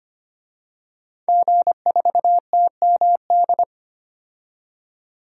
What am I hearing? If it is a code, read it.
Morse code, heard as a keyed tone, about 25 wpm, 700 Hz. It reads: G4TMD
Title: DXCC-100 Call Signs - 5 Characters - 25wpm